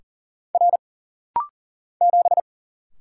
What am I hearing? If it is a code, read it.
Morse code, heard as a keyed tone, about 40 wpm, 700 Hz. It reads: R 7
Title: Single Letter-Number - Rapid-Fire 40wpm